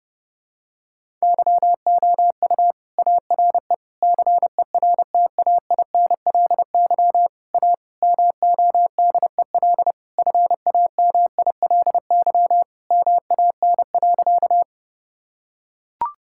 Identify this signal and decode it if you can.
Morse code, heard as a keyed tone, about 32 wpm, 700 Hz. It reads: YOU ARE CERTAINLY A MODEL FAMILY MAN.
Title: Sentences from Top 1000 Words - Review Part 5 - 30wpm